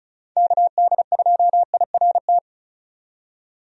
Morse code, heard as a keyed tone, about 36 wpm, 700 Hz. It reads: KD2IRT
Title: USA Call Signs - 6 Characters - 35wpm